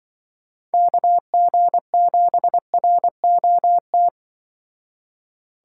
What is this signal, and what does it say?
Morse code, heard as a keyed tone, about 25 wpm, 700 Hz. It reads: KG7ROT
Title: Speed Racing - USA 2x3 Call Signs 20wpm